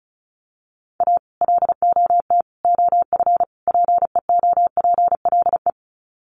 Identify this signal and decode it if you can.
Morse code, heard as a keyed tone, about 34 wpm, 700 Hz. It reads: A LOT OF PEOPLE
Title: Sets of 4 Words 35wpm